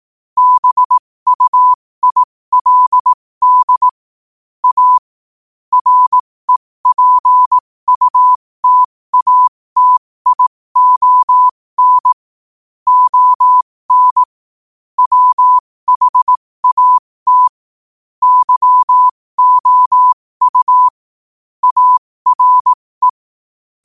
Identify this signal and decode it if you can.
Morse code, heard as a keyed tone, about 18 wpm, spaced as for 15 wpm, 1000 Hz. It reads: BUILD A REPUTATION ON WHAT YOU ARE
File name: qotdfiles / 15wpm